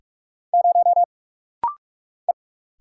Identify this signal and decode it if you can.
Morse code, heard as a keyed tone, about 44 wpm, 700 Hz. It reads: 0 E
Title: Single Letter-Number Alt - Rapid-Fire 45wpm